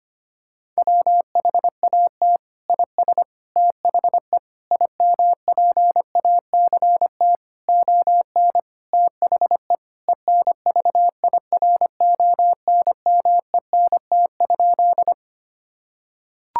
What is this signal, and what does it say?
Morse code, heard as a keyed tone, about 25 wpm, 700 Hz. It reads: WHAT IS THE IMPACT ON THE ENVIRONMENT?
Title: Sentences from Top 1000 Words - Review Part 6 - 25wpm